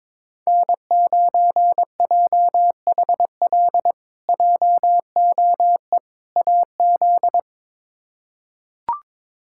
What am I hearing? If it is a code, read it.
Morse code, heard as a keyed tone, about 22 wpm, 700 Hz. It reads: N9JHL JOE AZ
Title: K1USN SST QSO Practice - 22wpm